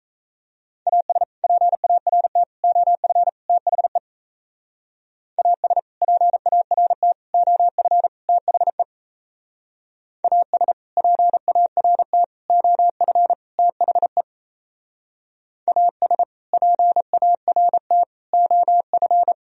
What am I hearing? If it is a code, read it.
Morse code, heard as a keyed tone, about 36 wpm, 700 Hz. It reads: AS PART OF THE AS PART OF THE AS PART OF THE AS PART OF